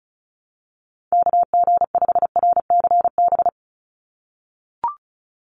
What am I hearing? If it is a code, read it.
Morse code, heard as a keyed tone, about 36 wpm, 700 Hz. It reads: KG5RCB
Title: USA Call Signs 35wpm